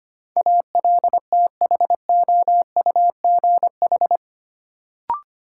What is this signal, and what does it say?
Morse code, heard as a keyed tone, about 25 wpm, 700 Hz. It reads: ALTHOUGH
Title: Sentences from Top 600 Words - New Word Drill - 25wpm